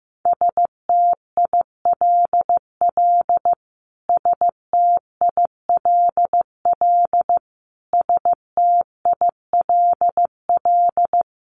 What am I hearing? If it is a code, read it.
Morse code, heard as a keyed tone, about 15 wpm, 700 Hz. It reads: STILL STILL STILL
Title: Top 500 Words Repeated 15wpm